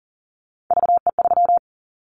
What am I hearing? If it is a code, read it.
Morse code, heard as a keyed tone, about 40 wpm, 700 Hz. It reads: VE3
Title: Callsign Prefixes for Top 25 DXCC - 40wpm